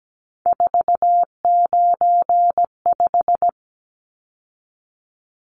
Morse code, heard as a keyed tone, about 17 wpm, 700 Hz. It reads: 495
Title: Numbers - 3-Digit 17wpm